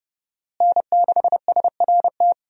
Morse code, heard as a keyed tone, about 32 wpm, 700 Hz. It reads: N6SRT
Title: USA Call Signs - 5 Characters - Repeated 2x - 30wpm